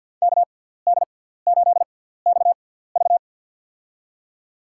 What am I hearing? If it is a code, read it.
Morse code, heard as a keyed tone, about 51 wpm, spaced as for 17 wpm, 700 Hz. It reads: KD8XV
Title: USA Call Signs - 5 Characters - 6x Character Spacing - 50wpm